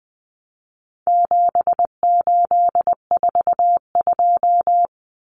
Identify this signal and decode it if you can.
Morse code, heard as a keyed tone, about 20 wpm, 700 Hz. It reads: 7842
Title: Numbers - 4-Digit 20wpm